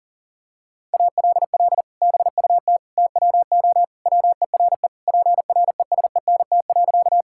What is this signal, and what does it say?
Morse code, heard as a keyed tone, about 37 wpm, 700 Hz. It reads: ALL BUT TWO WERE PRESENT.
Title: Sentences from Top 1000 Words - Review Part 5 - 40wpm